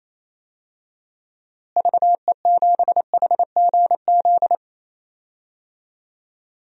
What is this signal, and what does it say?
Morse code, heard as a keyed tone, about 28 wpm, 700 Hz. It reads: VE7HGZ
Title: DXCC-100 Call Signs - 6 Characters - Repeated 2x - 28wpm